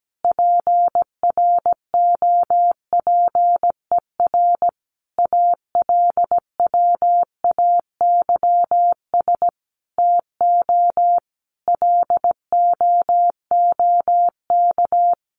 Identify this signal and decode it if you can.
Morse code, heard as a keyed tone, about 17 wpm, 700 Hz. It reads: PROPER ALWAYS TO LOOK
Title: Sentences from Top 1500 Words 17wpm